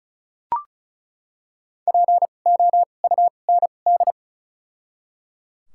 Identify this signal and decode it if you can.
Morse code, heard as a keyed tone, about 36 wpm, spaced as for 25 wpm, 700 Hz. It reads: POUND
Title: 2x Character Spacing - Top 500 Words - 35wpm